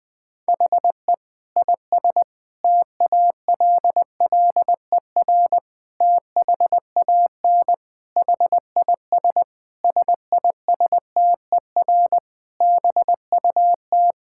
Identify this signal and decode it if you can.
Morse code, heard as a keyed tone, about 20 wpm, 700 Hz. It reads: HE IS TALLER THAN HIS SISTER BUT
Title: Difficult Sentences with Hints - 20wpm